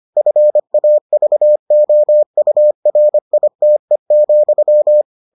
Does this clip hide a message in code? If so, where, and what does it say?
Yes: Morse code from 0.2 to 5.0 s, heard as a keyed tone, about 25 wpm, 600 Hz. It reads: FAVOURITE,